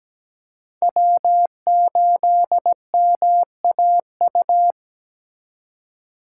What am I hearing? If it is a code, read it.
Morse code, heard as a keyed tone, about 17 wpm, 700 Hz. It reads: W8MAU
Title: USA Call Signs - 5 Characters - 3x Character Spacing - 17wpm